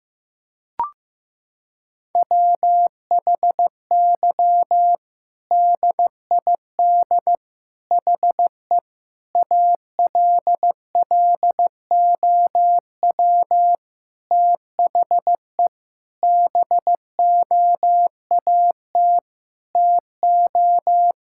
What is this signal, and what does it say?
Morse code, heard as a keyed tone, about 15 wpm, 700 Hz. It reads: WHY DID HE ALLOW THE BOAT TO
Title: Sentences from Top 1000 Words - Review Part 5 - 15wpm